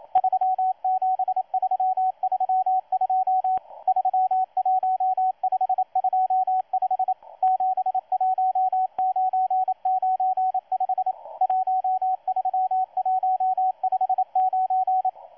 Morse code, heard as a keyed tone, about 27 wpm, 750 Hz, with radio noise. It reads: 37332 31525 71995 13159
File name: M12-2016-07-07-1310utc-13926khz-error-restart